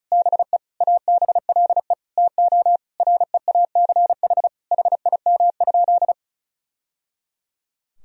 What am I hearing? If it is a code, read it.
Morse code, heard as a keyed tone, about 36 wpm, 700 Hz. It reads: BE ABLE TO REACH HIM?
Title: Sentences from Top 1000 Words - Review Part 2 - 35wpm